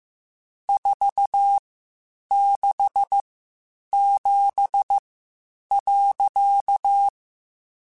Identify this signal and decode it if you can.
Morse code, heard as a keyed tone, about 15 wpm, spaced as for 8 wpm, 800 Hz. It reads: 467.
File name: (spaced) / QSOs Spaced-mp3s / kc7haa @13 sp